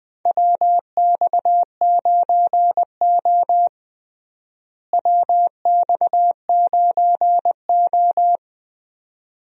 Morse code, heard as a keyed tone, about 20 wpm, 700 Hz. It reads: WX9O WX9O
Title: USA Call Signs - 4 Characters - Repeated 2x - 20wpm